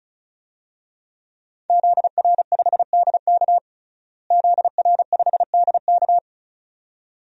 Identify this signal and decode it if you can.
Morse code, heard as a keyed tone, about 36 wpm, 700 Hz. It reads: ZR5DK ZR5DK
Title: DXCC-100 Call Signs - 5 Characters - Repeated 2x - 35wpm